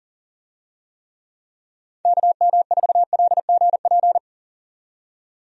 Morse code, heard as a keyed tone, about 40 wpm, 700 Hz. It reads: KM4LGP
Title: USA Call Signs - 6 Characters - 40wpm